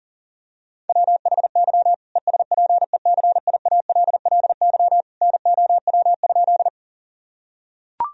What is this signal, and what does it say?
Morse code, heard as a keyed tone, about 40 wpm, 700 Hz. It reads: WHY ESPECIALLY NOW?
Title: Sentences from Top 1000 Words - Review Part 4 - 40wpm